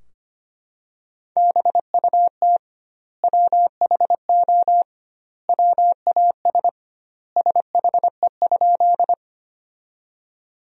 Morse code, heard as a keyed tone, about 25 wpm, 700 Hz. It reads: BUT WHO WAS SHE?
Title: 2x Word-Spacing - Sentences from Top 100 Words - 25wpm